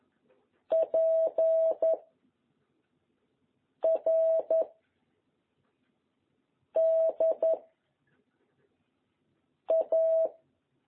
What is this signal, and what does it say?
Morse code, heard as a keyed tone, about 11 wpm, 660 Hz. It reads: P R D A
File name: jh8cbh / 2021cw